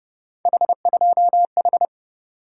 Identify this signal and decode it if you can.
Morse code, heard as a keyed tone, about 30 wpm, 700 Hz. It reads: H2H